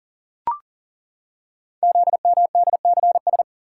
Morse code, heard as a keyed tone, about 40 wpm, 700 Hz. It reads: ZMDCS
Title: Code Groups - 5 Letters - 40wpm